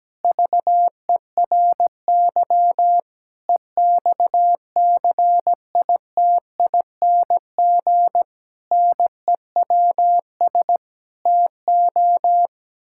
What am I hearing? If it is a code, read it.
Morse code, heard as a keyed tone, about 17 wpm, 700 Hz. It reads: VERY EXCITING NEWS TO
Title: Pangram Challenge v2 - 17wpm